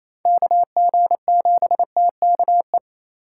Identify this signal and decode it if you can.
Morse code, heard as a keyed tone, about 34 wpm, 700 Hz. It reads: KG7TKE
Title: Speed Racing - USA 2x3 Call Signs 28wpm